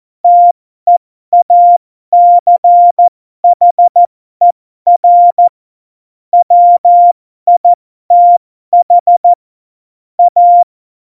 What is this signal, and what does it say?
Morse code, heard as a keyed tone, about 14 wpm, spaced as for 12 wpm, 700 Hz. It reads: TEACHER WITH A
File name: Morsekurs / Witze